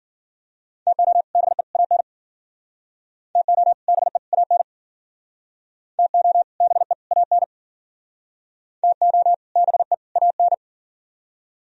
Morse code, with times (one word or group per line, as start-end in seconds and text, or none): 0.9-1.2 TO
1.4-1.6 BE
1.8-2.0 AN
3.4-3.7 TO
3.9-4.2 BE
4.3-4.6 AN
6.0-6.4 TO
6.6-6.9 BE
7.1-7.4 AN
8.8-9.3 TO
9.6-9.9 BE
10.2-10.5 AN